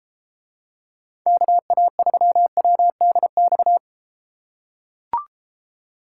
Morse code, heard as a keyed tone, about 32 wpm, 700 Hz. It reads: KA3WDX